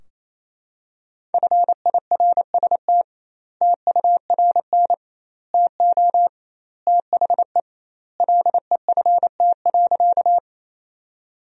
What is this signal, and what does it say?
Morse code, heard as a keyed tone, about 28 wpm, 700 Hz. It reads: FIRST TURN TO THE LEFT.
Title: 2x Word-Spacing - Sentences from Top 300 Words - 28wpm